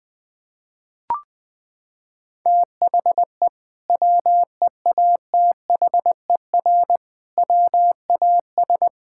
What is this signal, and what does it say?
Morse code, heard as a keyed tone, about 20 wpm, 700 Hz. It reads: THE WEATHER WAS
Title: Sentences from Top 1000 Words - Review Part 5 - 20wpm